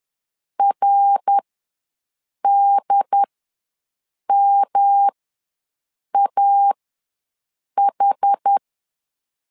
Morse code, heard as a keyed tone, about 10 wpm, spaced as for 6 wpm, 800 Hz. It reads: RDMAH
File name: LEZIONI / 3 MWRO / 3 MWR ws 10wpm 2151